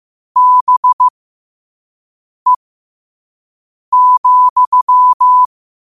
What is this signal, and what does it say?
Morse code, heard as a keyed tone, about 15 wpm, spaced as for 5 wpm, 1000 Hz. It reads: BE,
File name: 5wpm